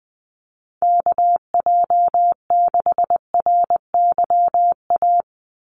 Morse code, heard as a keyed tone, about 20 wpm, 700 Hz. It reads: KJ6RYA
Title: USA Call Signs 20wpm